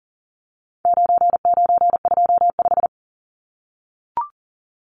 Morse code, heard as a keyed tone, about 40 wpm, 700 Hz. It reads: 9925
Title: Numbers - 4-Digit 40wpm